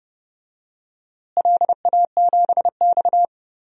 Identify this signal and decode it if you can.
Morse code, heard as a keyed tone, about 31 wpm, 700 Hz. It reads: LA7X
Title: DXCC-100 Call Signs - 4 Characters - Repeated 3x - 30wpm